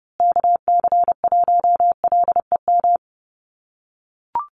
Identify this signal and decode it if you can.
Morse code, heard as a keyed tone, about 30 wpm, 700 Hz. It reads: KC1LEM
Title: USA Call Signs Repeated 3x 30wpm